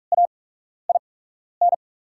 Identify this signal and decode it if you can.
Morse code, heard as a keyed tone, about 44 wpm, spaced as for 12 wpm, 700 Hz. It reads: AIN
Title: 8x Character Spacing - Top 200 Words - 45wpm